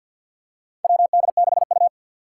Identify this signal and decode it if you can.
Morse code, heard as a keyed tone, about 51 wpm, 700 Hz. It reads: WD6U
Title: USA Call Signs - 4 Characters - 3x Character Spacing - 50wpm